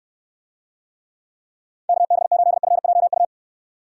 Morse code, heard as a keyed tone, about 69 wpm, 700 Hz. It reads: KB8FJU